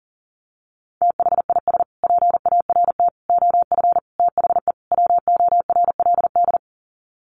Morse code, heard as a keyed tone, about 40 wpm, 700 Hz. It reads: THIS PART OF THE WORLD
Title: Sets of 5 Words Encore2 40wpm